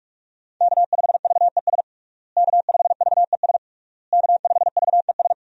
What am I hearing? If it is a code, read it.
Morse code, heard as a keyed tone, about 46 wpm, 700 Hz. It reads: K5VES K5VES K5VES
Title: USA Call Signs - 5 Characters - Repeated 3x - 45wpm